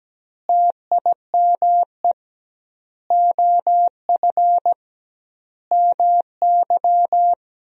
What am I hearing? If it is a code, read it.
Morse code, heard as a keyed tone, about 17 wpm, 700 Hz. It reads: TIME OF MY
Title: 2x Word-Spacing - Sentences from Top 300 Words - 17wpm